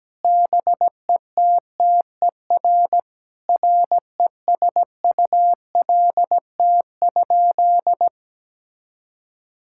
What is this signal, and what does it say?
Morse code, heard as a keyed tone, about 17 wpm, 700 Hz. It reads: BETTER RESULT?
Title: Sentences from Top 1000 Words - Review Part 6 - 17wpm